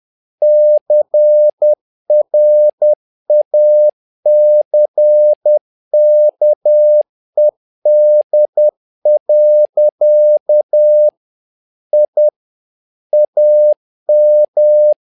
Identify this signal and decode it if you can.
Morse code, heard as a keyed tone, about 10 wpm, 600 Hz. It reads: CRACKED. I AM
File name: learning center / tower trouble / tower trouble 10wpm